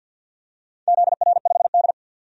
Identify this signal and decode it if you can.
Morse code, heard as a keyed tone, about 51 wpm, 700 Hz. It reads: ZR5D
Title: DXCC-100 Call Signs - 4 Characters - 50wpm